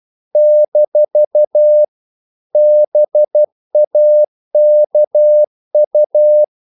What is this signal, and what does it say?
Morse code, heard as a keyed tone, about 12 wpm, 600 Hz. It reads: - BAKU
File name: CW / Practice / NationalCapitals12